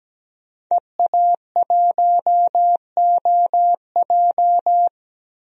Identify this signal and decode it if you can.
Morse code, heard as a keyed tone, about 17 wpm, 700 Hz. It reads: EA1OJ
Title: DXCC-100 Call Signs - 5 Characters - 17wpm